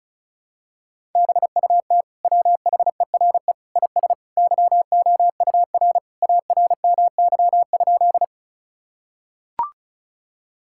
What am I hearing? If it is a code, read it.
Morse code, heard as a keyed tone, about 36 wpm, 700 Hz. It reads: BUT WHERE IS YOUR ARMY?
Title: Sentences from Top 1000 Words - Review Part 5 - 35wpm